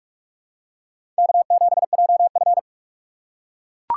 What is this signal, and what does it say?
Morse code, heard as a keyed tone, about 43 wpm, 700 Hz. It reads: K7JF
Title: USA Call Signs - 2x Character Spacing - 45wpm